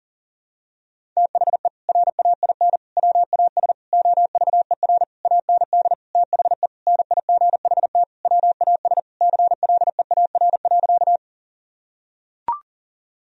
Morse code, heard as a keyed tone, about 40 wpm, 700 Hz. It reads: THE RAIN WAS OVER AND THE NIGHT WAS CLEAR.
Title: Sentences from Top 1000 Words - Review Part 1 - 40wpm